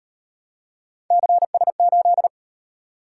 Morse code, heard as a keyed tone, about 38 wpm, 700 Hz. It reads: CS8